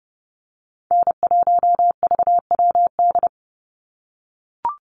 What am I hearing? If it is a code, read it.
Morse code, heard as a keyed tone, about 30 wpm, 700 Hz. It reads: N1VWD
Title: NA SOTA Chaser Callsigns 30wpm